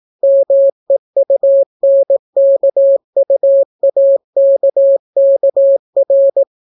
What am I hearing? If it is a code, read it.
Morse code, heard as a keyed tone, about 18 wpm, 550 Hz. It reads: MEUNKUAKKR